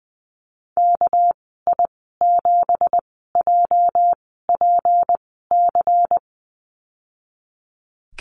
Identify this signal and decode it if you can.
Morse code, heard as a keyed tone, about 20 wpm, spaced as for 15 wpm, 700 Hz. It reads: KI7JPC
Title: 207 Lesson 33g - Letter j - Call Signs 20wpm